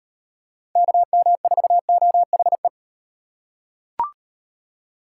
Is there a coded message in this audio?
Yes: Morse code from 0.8 to 2.7 s, heard as a keyed tone, about 38 wpm, 700 Hz. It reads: KM4OHE